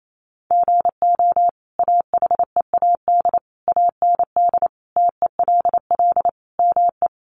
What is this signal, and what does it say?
Morse code, heard as a keyed tone, about 28 wpm, 700 Hz. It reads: GO AHEAD AND TELL ME
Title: Sentences from Top 1100 Words 28wpm